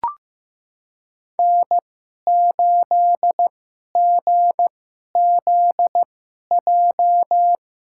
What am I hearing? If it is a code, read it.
Morse code, heard as a keyed tone, about 15 wpm, spaced as for 11 wpm, 700 Hz. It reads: N8GZJ
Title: USA Call Signs - 5 Characters - 2x Character Spacing - 15wpm